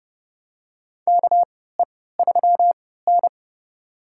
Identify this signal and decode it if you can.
Morse code, heard as a keyed tone, about 30 wpm, spaced as for 17 wpm, 700 Hz. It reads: KE3N
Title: USA Call Signs - 4 Characters - 3x Character Spacing - 30wpm